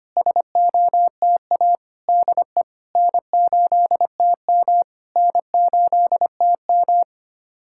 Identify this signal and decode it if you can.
Morse code, heard as a keyed tone, about 25 wpm, 700 Hz. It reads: SOTA DE N8TM N8TM
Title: SOTA QSO Practice 25wpm